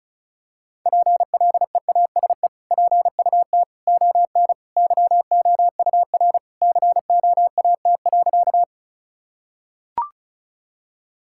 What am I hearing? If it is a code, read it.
Morse code, heard as a keyed tone, about 36 wpm, 700 Hz. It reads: PLEASE PUT ON YOUR COAT.
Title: Sentences from Top 1600 Words 35wpm